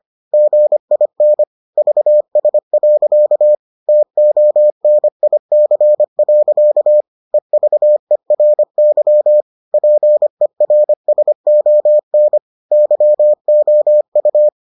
Morse code, heard as a keyed tone, about 25 wpm, 600 Hz. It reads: GIN VS. TONIC. EVERY PERSON YOU